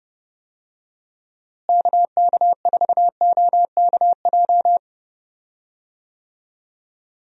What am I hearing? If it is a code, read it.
Morse code, heard as a keyed tone, about 29 wpm, 700 Hz. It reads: KK4OKJ
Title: USA Call Signs - 6 Characters - Repeated 2x - 30wpm